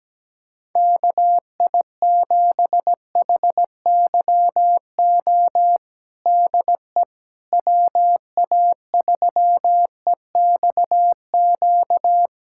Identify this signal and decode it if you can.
Morse code, heard as a keyed tone, about 17 wpm, 700 Hz. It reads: KI7HYO DE WA3EXQ
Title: USA Callsign DE Callsign - 6 Characters - 17wpm